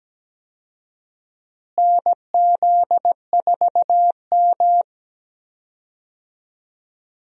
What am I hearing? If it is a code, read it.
Morse code, heard as a keyed tone, about 17 wpm, 700 Hz. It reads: NZ4M
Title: USA Call Signs - 4 Characters - Repeated 2x - 17wpm